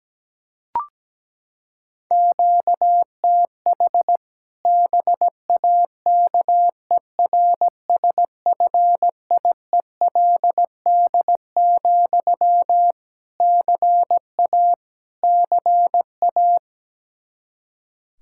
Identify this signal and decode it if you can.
Morse code, heard as a keyed tone, about 17 wpm, 700 Hz. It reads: QTH BAKERSFIELD, CA CA
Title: Practice QSO - Exchange 1 - Beginning - 17wpm